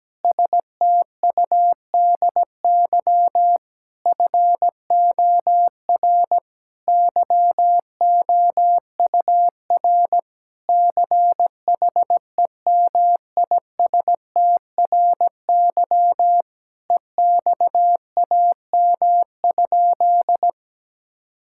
Morse code, heard as a keyed tone, about 17 wpm, 700 Hz. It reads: STUDY FOR YOUR CHEMISTRY EXAM?
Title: Sentences from Top 1000 Words - Review Part 6 - 17wpm